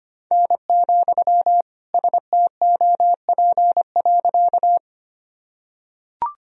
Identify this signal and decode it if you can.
Morse code, heard as a keyed tone, about 25 wpm, 700 Hz. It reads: N, STOP.
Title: Sentences from Top 300 Words and Abbreviations 25wpm